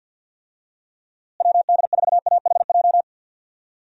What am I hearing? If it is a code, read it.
Morse code, heard as a keyed tone, about 53 wpm, 700 Hz. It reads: WD4AHJ
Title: USA Call Signs - 6 Characters - Repeated 2x - 50wpm